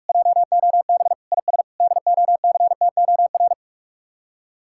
Morse code, heard as a keyed tone, about 49 wpm, 700 Hz. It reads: JOB IS DOCTOR